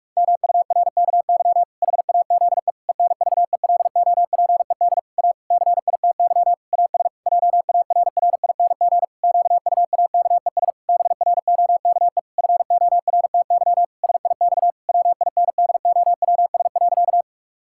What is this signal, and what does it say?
Morse code, heard as a keyed tone, about 41 wpm, 700 Hz. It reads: MURKY HAZE ENVELOPED A CITY AS JARRING QUAKES BROKE FORTY SIX WINDOWS.
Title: Pangram Challenge v2 - 45wpm